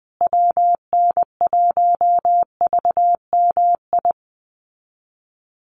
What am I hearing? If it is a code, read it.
Morse code, heard as a keyed tone, about 20 wpm, 700 Hz. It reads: WN1VMI
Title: USA Call Signs Repeated 3x 20wpm